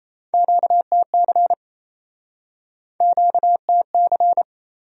Morse code, heard as a keyed tone, about 38 wpm, 700 Hz. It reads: QTC QTC